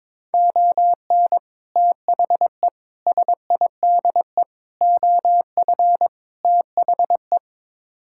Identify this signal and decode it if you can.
Morse code, heard as a keyed tone, about 22 wpm, 700 Hz. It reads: ON THE SIDE OF THE
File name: practice / 22wpm / Build A Phrase - 5-Word Phrases - 22wpm